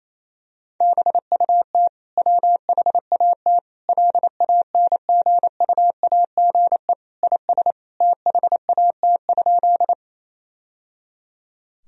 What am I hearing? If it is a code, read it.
Morse code, heard as a keyed tone, about 28 wpm, 700 Hz. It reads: BUT WHAT LANGUAGE IS THAT?
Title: Sentences from Top 1000 Words - Review Part 3 - 28wpm